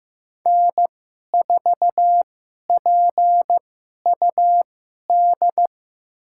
Morse code, heard as a keyed tone, about 15 wpm, spaced as for 11 wpm, 700 Hz. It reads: N4PUD
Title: USA Call Signs - 5 Characters - 2x Character Spacing - 15wpm